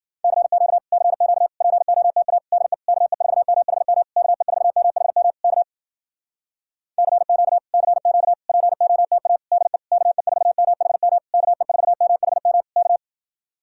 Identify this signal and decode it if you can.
Morse code, heard as a keyed tone, about 52 wpm, 700 Hz. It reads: CQ CQ POTA DE KE4MHM KE4MHM K CQ CQ POTA DE KE4MHM KE4MHM K